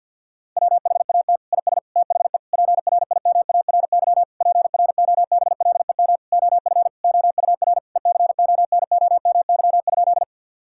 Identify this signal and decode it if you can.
Morse code, heard as a keyed tone, about 52 wpm, 700 Hz. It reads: WHAT IS THE PRIMARY PROBLEM OF OUR ECONOMY?
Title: Sentences from Top 1000 Words - Review Part 5 - 50wpm